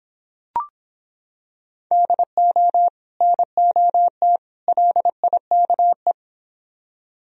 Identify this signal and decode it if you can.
Morse code, heard as a keyed tone, about 26 wpm, 700 Hz. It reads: DO NOT LIKE